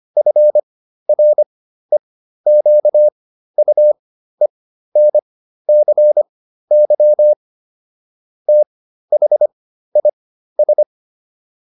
Morse code, heard as a keyed tone, about 25 wpm, spaced as for 13 wpm, 600 Hz. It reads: FREQUENCY THIS